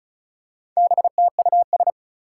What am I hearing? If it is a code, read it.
Morse code, heard as a keyed tone, about 36 wpm, 700 Hz. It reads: BTUS
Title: Four Challenging Letter-Number Combinations 35wpm